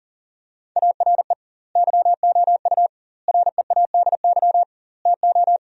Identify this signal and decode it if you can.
Morse code, heard as a keyed tone, about 37 wpm, 700 Hz. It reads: ARE YOU READY TO
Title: 2x Word-Spacing - Sentences from Top 400 Words - 40wpm